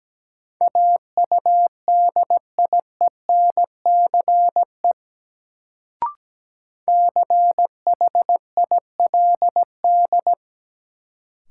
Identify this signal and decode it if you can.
Morse code, heard as a keyed tone, about 17 wpm, 700 Hz. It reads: AUDIENCE CHILD
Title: Sentences from Top 800 Words - New Word Drill - 17wpm